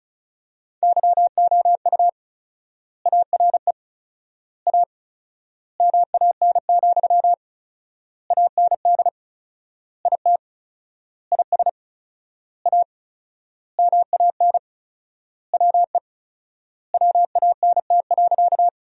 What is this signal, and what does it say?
Morse code, heard as a keyed tone, about 34 wpm, 700 Hz. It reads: YOU ARE A MAN, AND IT IS A MAN WE WANT.
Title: 4x Word-Spacing - Sentences from Top 200 Words - 35wpm